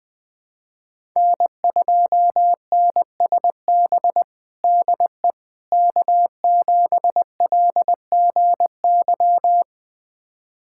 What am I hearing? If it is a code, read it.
Morse code, heard as a keyed tone, about 20 wpm, 700 Hz. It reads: N2NSB DE K7LGY
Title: USA Callsign DE Callsign - 5 Characters - 20wpm